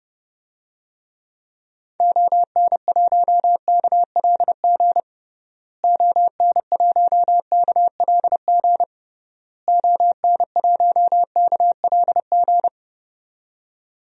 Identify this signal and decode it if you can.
Morse code, heard as a keyed tone, about 30 wpm, 700 Hz. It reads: ON1KLG ON1KLG ON1KLG
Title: DXCC-100 Call Signs - 6 Characters - Repeated 3x - 30wpm